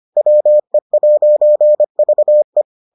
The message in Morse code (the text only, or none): WE'VE